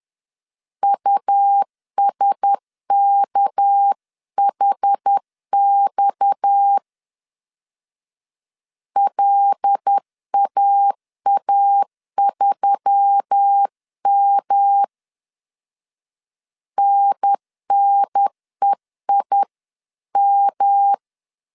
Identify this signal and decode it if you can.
Morse code, heard as a keyed tone, about 10 wpm, 800 Hz. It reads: USKHX LAA3M NNEIM